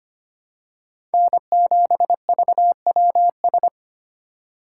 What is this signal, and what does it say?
Morse code, heard as a keyed tone, about 25 wpm, 700 Hz. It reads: N7VWS
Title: USA Call Signs - 5 Characters - 2x Character Spacing - 25wpm